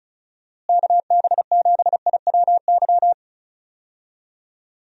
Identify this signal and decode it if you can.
Morse code, heard as a keyed tone, about 34 wpm, 700 Hz. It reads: KB7IWY
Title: USA Call Signs - 6 Characters - Repeated 3x - 35wpm